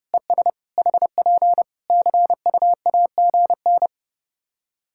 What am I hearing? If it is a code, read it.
Morse code, heard as a keyed tone, about 28 wpm, 700 Hz. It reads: ES HP CUAGN
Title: Practice QSO - Exchange 4 - Ending - 30wpm